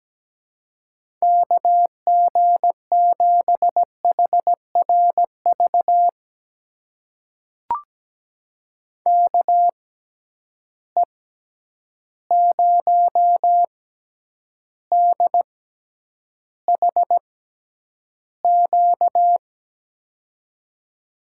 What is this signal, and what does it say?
Morse code, heard as a keyed tone, about 17 wpm, spaced as for 9 wpm, 700 Hz. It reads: KG7HRV KE0DHQ
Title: USA Call Signs - 6 Characters - 6x Character Spacing - 17wpm